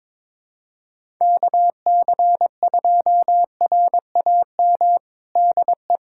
Morse code, heard as a keyed tone, about 21 wpm, 700 Hz. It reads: KC2RAM DE
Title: USA Callsign DE Callsign - 6 Characters - 22wpm